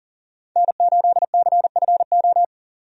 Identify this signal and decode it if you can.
Morse code, heard as a keyed tone, about 40 wpm, 700 Hz. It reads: N8CFO
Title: USA Call Signs - 5 Characters - 40wpm